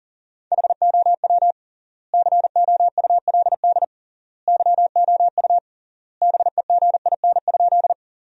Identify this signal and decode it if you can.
Morse code, heard as a keyed tone, about 40 wpm, 700 Hz. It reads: HOW COULD YOU BEGIN?
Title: 3x Word-Spacing - Sentences from Top 300 Words - 40wpm